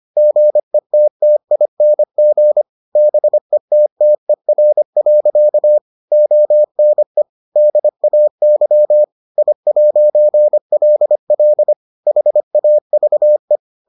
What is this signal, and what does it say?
Morse code, heard as a keyed tone, about 25 wpm, 600 Hz. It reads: GETTING BETTER. ONE DAY I'LL HAVE